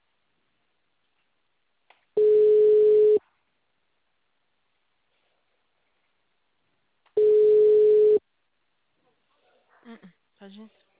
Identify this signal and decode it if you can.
Ringback tone (Europe, 425 Hz), with speech.